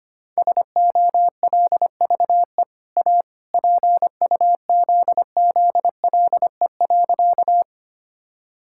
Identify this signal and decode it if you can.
Morse code, heard as a keyed tone, about 25 wpm, 700 Hz. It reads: SOLVE A PUZZLE.
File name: Easy Sentences with Hints - 25wpm